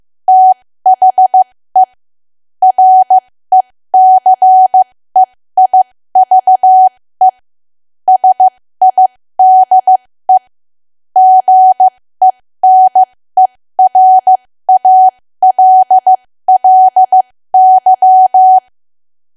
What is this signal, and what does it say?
Morse code, heard as a keyed tone, about 15 wpm, spaced as for 13 wpm, 750 Hz. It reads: THE RECEIVE SIDE GENERALLY
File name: cwa / W1AW-Level2 / 13WPM